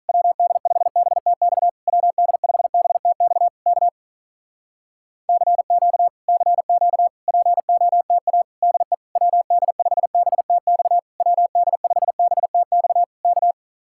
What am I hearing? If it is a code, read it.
Morse code, heard as a keyed tone, about 46 wpm, 700 Hz. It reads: WD5BTX WD5BTX K CQ CQ POTA DE WD5BTX WD5BTX K